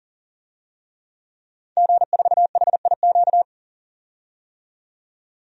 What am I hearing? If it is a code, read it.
Morse code, heard as a keyed tone, about 40 wpm, 700 Hz. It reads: G4HIQ
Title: DXCC-100 Call Signs - 5 Characters - 40wpm